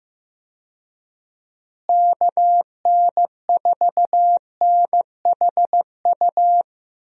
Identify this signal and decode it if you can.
Morse code, heard as a keyed tone, about 15 wpm, 700 Hz. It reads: KN4NHU
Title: USA Call Signs - 6 Characters - Repeated 2x - 15wpm